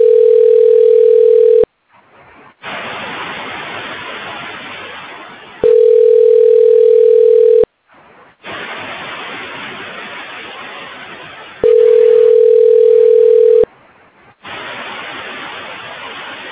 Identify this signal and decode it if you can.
Ringback tone (North America), 440 + 480 Hz.